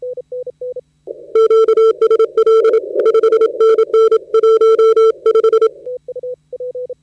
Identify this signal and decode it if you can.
Morse code, heard as a keyed tone, about 27 wpm, 440 Hz. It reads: QSL 5NN 15
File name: Audio.CQWW.CW.2013.from.KB7Q / 10minute snippet PJ2T 40M KB7Q op